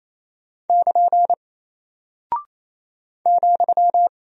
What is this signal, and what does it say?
Morse code, heard as a keyed tone, about 28 wpm, 700 Hz. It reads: <KN> ,
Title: Single Character (Letter:Number:Punctuation:Prosign) 28wpm